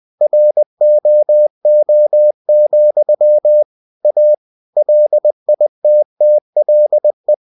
Morse code, heard as a keyed tone, about 20 wpm, 600 Hz. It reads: ROO, A LITTLE
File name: cw-books / winnie-the-pooh / winnie-20wpm-page-0108